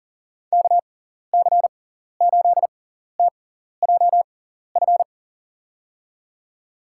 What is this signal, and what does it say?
Morse code, heard as a keyed tone, about 40 wpm, spaced as for 14 wpm, 700 Hz. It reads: KC8TJF
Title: USA Call Signs - 6 Characters - 6x Character Spacing - 40wpm